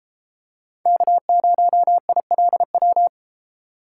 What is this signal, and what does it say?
Morse code, heard as a keyed tone, about 31 wpm, 700 Hz. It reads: K0ILW